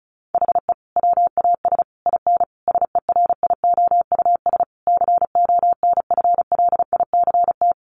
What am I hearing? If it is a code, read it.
Morse code, heard as a keyed tone, about 34 wpm, 700 Hz. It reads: HE WAS IN SERIOUS CONFLICT
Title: Sentences from Top 1300 Words 35wpm